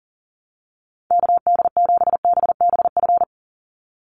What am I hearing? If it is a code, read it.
Morse code, heard as a keyed tone, about 40 wpm, 700 Hz. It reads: KD7BBF
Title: USA Call Signs 40wpm